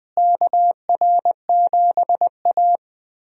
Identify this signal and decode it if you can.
Morse code, heard as a keyed tone, about 20 wpm, 700 Hz. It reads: KR7A
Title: USA Call Signs - 4 Characters - Repeated 2x - 20wpm